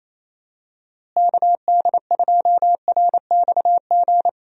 Morse code, heard as a keyed tone, about 28 wpm, 700 Hz. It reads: KD2RXG
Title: USA Call Signs - 6 Characters - Repeated 2x - 28wpm